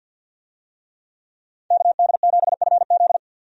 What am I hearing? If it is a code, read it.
Morse code, heard as a keyed tone, about 51 wpm, 700 Hz. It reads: KD7FZ
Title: USA Call Signs - 5 Characters - 50wpm